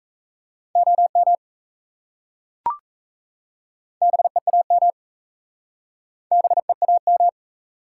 Morse code, heard as a keyed tone, about 37 wpm, 700 Hz. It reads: OM BEAM BEAM